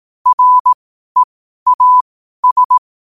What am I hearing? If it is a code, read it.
Morse code, heard as a keyed tone, about 18 wpm, spaced as for 12 wpm, 1000 Hz. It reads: REAS
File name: qotdfiles / 12wpm